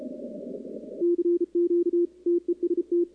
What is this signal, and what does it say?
Morse code, heard as a keyed tone, about 34 wpm, 340 Hz, with radio noise. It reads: CQ TEST
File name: contests / 2019-cw-wpx-kt5j-k5pi- -2330 0-11